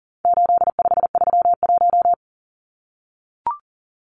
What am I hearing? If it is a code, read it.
Morse code, heard as a keyed tone, about 40 wpm, 700 Hz. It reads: 8531
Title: Numbers - 4-Digit 40wpm